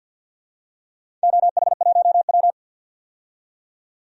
Morse code, heard as a keyed tone, about 47 wpm, 700 Hz. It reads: OH1W
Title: DXCC-100 Call Signs - 4 Characters - 50wpm